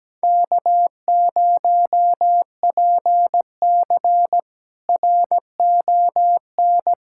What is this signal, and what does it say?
Morse code, heard as a keyed tone, about 17 wpm, 700 Hz. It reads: K0PC RON
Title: K1USN SST QSO Practice - 17wpm